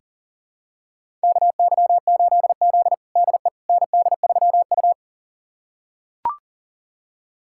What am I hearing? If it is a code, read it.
Morse code, heard as a keyed tone, about 40 wpm, 700 Hz. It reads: KY8Z DE ND3U
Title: USA Callsign DE Callsign - 4 Characters - 40wpm